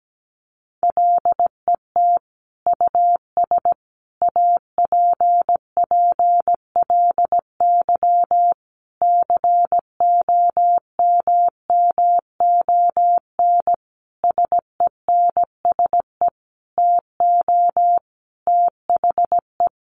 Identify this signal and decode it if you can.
Morse code, heard as a keyed tone, about 17 wpm, 700 Hz. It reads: LET US APPLY COMMON SENSE TO THE
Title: Sentences from Top 1200 Words 17wpm